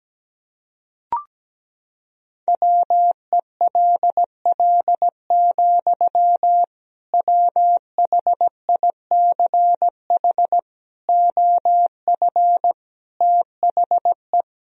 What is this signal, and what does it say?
Morse code, heard as a keyed tone, about 17 wpm, 700 Hz. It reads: WELL, WHICH OF THE
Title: Sentences from Top 1000 Words - Review Part 1 - 17wpm